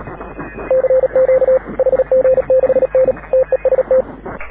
Morse code, heard as a keyed tone, about 36 wpm, 530 Hz, with radio noise. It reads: CQ HG6N TEST
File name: AUDIO2019 / HG6N CW SSB MIX